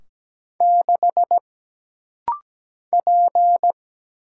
Morse code, heard as a keyed tone, about 17 wpm, 700 Hz. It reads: 6 P
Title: Single Letter-Number - Rapid-Fire 17wpm